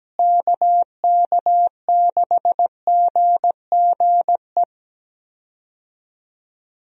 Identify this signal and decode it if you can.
Morse code, heard as a keyed tone, about 17 wpm, 700 Hz. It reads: KK6GGE
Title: USA Call Signs - 6 Characters - Repeated 2x - 17wpm